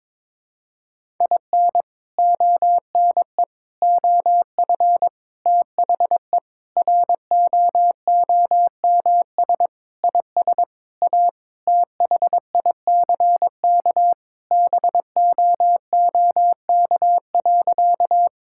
Morse code, heard as a keyed tone, about 22 wpm, 700 Hz. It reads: IN ONE OF THE ROOMS IS A THICK BOOK.
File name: practice / 22wpm / Sentences with 4-Word Phrases - 22wpm